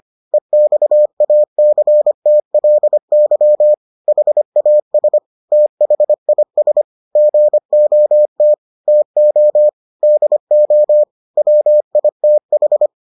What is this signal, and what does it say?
Morse code, heard as a keyed tone, about 25 wpm, 600 Hz. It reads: EXACTLY HAS THIS GOT TO DO WITH